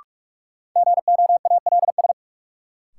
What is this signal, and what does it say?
Morse code, heard as a keyed tone, about 46 wpm, 700 Hz. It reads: GOALS
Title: Sentences from Top 1000 Words - Review Part 6 - New Word Drill - 45wpm